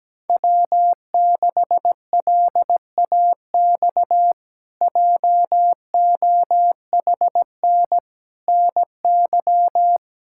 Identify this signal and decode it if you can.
Morse code, heard as a keyed tone, about 17 wpm, 700 Hz. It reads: W6LAX JOHN NY
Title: K1USN SST QSO Practice - 17wpm